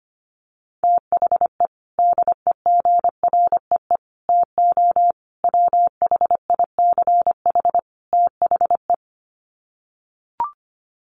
Morse code, heard as a keyed tone, about 25 wpm, 700 Hz. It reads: THE DEGREE TO WHICH THE
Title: Sets of 5 Words Encore2 25wpm